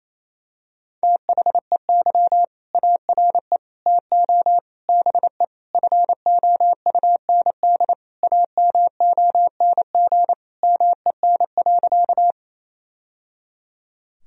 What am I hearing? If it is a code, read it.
Morse code, heard as a keyed tone, about 28 wpm, 700 Hz. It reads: THEY ARE TO BE FOUND AMONG MEN.
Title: Sentences from Top 1000 Words - Review Part 3 - 28wpm